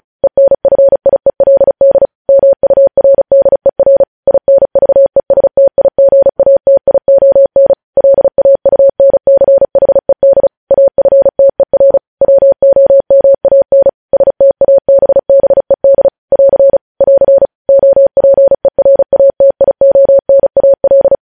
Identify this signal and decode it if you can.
Morse code, heard as a keyed tone, about 36 wpm, 550 Hz. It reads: ENFIELD MURDER INVESTIGATION LAUNCHED AFTER WOMAN STABBED <AR> <AR> OPERATIONAL
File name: podcast / News headlines-1x-35x35